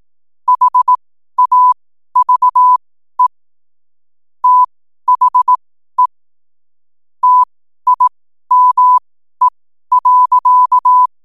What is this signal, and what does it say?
Morse code, heard as a keyed tone, about 18 wpm, spaced as for 12 wpm, 1000 Hz. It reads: HAVE THE TIME.
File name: n7net / 12wpm